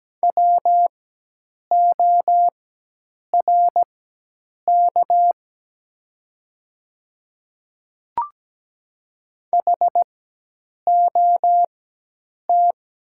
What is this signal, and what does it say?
Morse code, heard as a keyed tone, about 17 wpm, spaced as for 8 wpm, 700 Hz. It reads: WORK HOT
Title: 4x Character Spacing - Top 200 Words - 17wpm